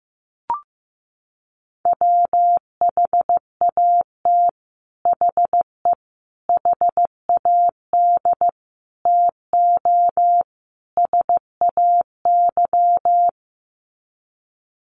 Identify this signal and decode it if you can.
Morse code, heard as a keyed tone, about 15 wpm, 700 Hz. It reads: WHAT HE HAD TO SAY
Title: Sets of 5 Words Encore2 15wpm